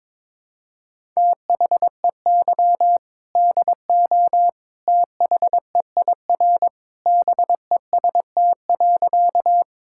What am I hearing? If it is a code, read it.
Morse code, heard as a keyed tone, about 22 wpm, 700 Hz. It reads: THEY DO THEIR BEST.
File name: practice / 22wpm / Sentences with 2-Word Phrases - 22wpm